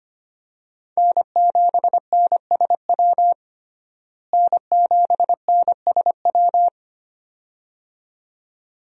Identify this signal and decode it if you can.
Morse code, heard as a keyed tone, about 25 wpm, 700 Hz. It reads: N7NSW N7NSW
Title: USA Call Signs - 5 Characters - Repeated 2x - 25wpm